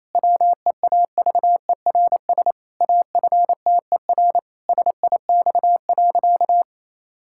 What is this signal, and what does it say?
Morse code, heard as a keyed tone, about 28 wpm, 700 Hz. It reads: WEAVERS AFTER SIX.
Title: Pangram Challenge v2 - 28wpm